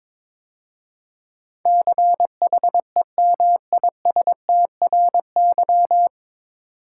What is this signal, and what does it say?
Morse code, heard as a keyed tone, about 22 wpm, 700 Hz. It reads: CHEMISTRY
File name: 9 Letter Words - First 3 Letters Given - 22wpm